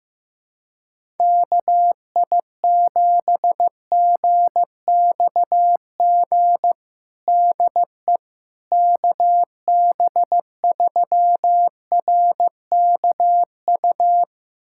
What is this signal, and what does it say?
Morse code, heard as a keyed tone, about 15 wpm, 700 Hz. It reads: KI7GXG DE KB3RKU
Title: USA Callsign DE Callsign 15wpm